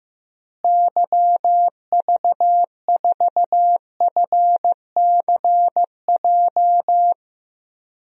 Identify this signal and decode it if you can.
Morse code, heard as a keyed tone, about 15 wpm, 700 Hz. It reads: YV4FCJ
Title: DXCC-100 Call Signs - 6 Characters - Repeated 3x - 15wpm